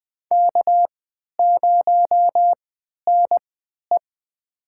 Morse code, heard as a keyed tone, about 20 wpm, spaced as for 11 wpm, 700 Hz. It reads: K0NE
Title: USA Call Signs - 4 Characters - 3x Character Spacing - 20wpm